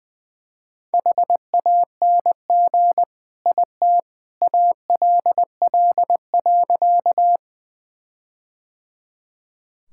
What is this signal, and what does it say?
Morse code, heard as a keyed tone, about 20 wpm, 700 Hz. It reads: HANG IT ALL.
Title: Sentences from Top 1600 Words 20wpm